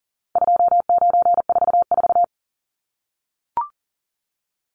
Morse code, heard as a keyed tone, about 40 wpm, 700 Hz. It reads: 2944
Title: Numbers - 4-Digit 40wpm